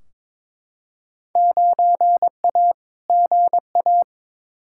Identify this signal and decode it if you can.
Morse code, heard as a keyed tone, about 22 wpm, 700 Hz. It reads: 9A GA
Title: Field Day Exchanges Encore - 22wpm